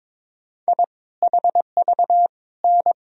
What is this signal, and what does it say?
Morse code, heard as a keyed tone, about 22 wpm, 700 Hz. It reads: I HV N
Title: Sentences from Top 400 Words and Abbreviations 22wpm